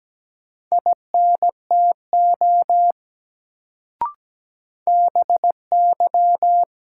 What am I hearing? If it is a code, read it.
Morse code, heard as a keyed tone, about 17 wpm, 700 Hz. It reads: INTO BY
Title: Sets of 2 Words - New Word Drill - 17wpm